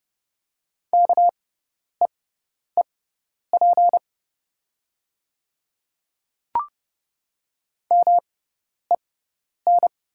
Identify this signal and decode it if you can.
Morse code, heard as a keyed tone, about 30 wpm, spaced as for 10 wpm, 700 Hz. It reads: KEEP MEN
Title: 6x Character Spacing - Top 300 Words - 30wpm